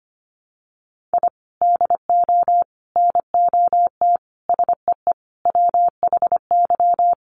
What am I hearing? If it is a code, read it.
Morse code, heard as a keyed tone, about 25 wpm, 700 Hz. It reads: I DO NOT SEE WHY
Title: Sets of 5 Words Encore 25wpm